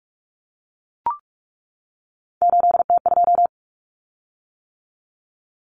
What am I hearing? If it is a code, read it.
Morse code, heard as a keyed tone, about 44 wpm, 700 Hz. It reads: 8T2
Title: Common Cut Numbers - 3-Digit 45wpm